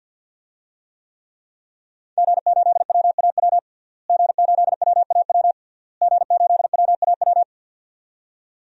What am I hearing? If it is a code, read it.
Morse code, heard as a keyed tone, about 49 wpm, 700 Hz. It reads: G8WAW G8WAW G8WAW
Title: DXCC-100 Call Signs - 5 Characters - Repeated 3x - 50wpm